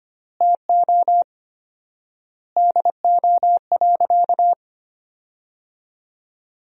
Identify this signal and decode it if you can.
Morse code, heard as a keyed tone, about 25 wpm, 700 Hz. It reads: TO DO.
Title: 4x Word-Spacing - Sentences from Top 100 Words - 25wpm